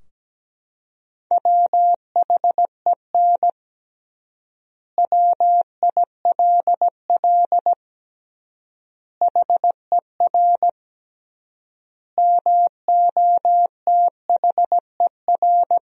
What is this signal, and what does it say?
Morse code, heard as a keyed tone, about 17 wpm, 700 Hz. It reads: WHEN WILL HER MOTHER
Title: 3x Word-Spacing - Sentences from Top 200 Words - 17wpm